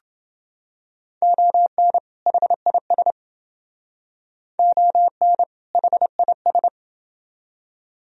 Morse code, heard as a keyed tone, about 28 wpm, 700 Hz. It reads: ON HIS ON HIS